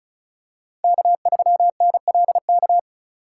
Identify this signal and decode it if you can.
Morse code, heard as a keyed tone, about 34 wpm, 700 Hz. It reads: K3NLK
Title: USA Call Signs - 5 Characters - 4x Character Spacing - 35wpm